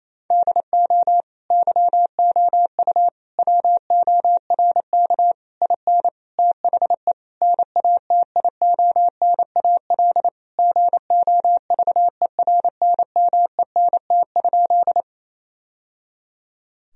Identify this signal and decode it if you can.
Morse code, heard as a keyed tone, about 28 wpm, 700 Hz. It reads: DO YOU WORK IN THE NATIONAL GOVERNMENT?
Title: Sentences from Top 1000 Words - Review Part 4 - 28wpm